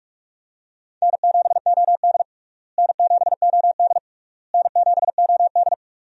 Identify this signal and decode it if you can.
Morse code, heard as a keyed tone, about 41 wpm, 700 Hz. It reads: N7OD N7OD N7OD
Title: USA Call Signs - 4 Characters - Repeated 3x - 45wpm